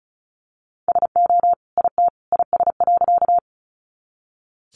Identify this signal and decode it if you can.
Morse code, heard as a keyed tone, about 34 wpm, 700 Hz. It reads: SO IT IS.
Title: Sentences from Top 100 Words 35wpm